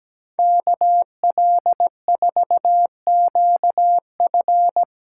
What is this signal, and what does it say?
Morse code, heard as a keyed tone, about 17 wpm, 700 Hz. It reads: KL4QF
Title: USA Callsign DE Callsign 17wpm